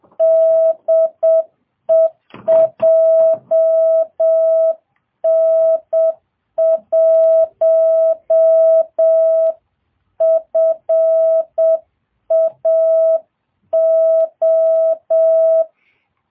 Morse code, heard as a keyed tone, about 7 wpm, 660 Hz. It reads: DE JN1FAO